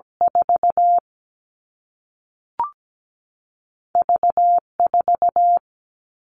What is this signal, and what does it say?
Morse code, heard as a keyed tone, about 17 wpm, 700 Hz. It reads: 4 V4
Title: All Cut Numbers - 2-Digit 17wpm